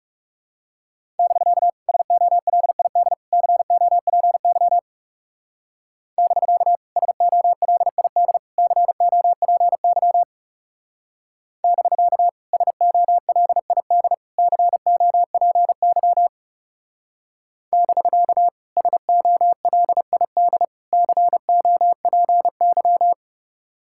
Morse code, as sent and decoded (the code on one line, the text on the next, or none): -...-.- / ... --- .-.. .. -.. / -.-. --- .--. -.-- / -...-.- / ... --- .-.. .. -.. / -.-. --- .--. -.-- / -...-.- / ... --- .-.. .. -.. / -.-. --- .--. -.-- / -...-.- / ... --- .-.. .. -.. / -.-. --- .--. -.--
<BK> SOLID COPY <BK> SOLID COPY <BK> SOLID COPY <BK> SOLID COPY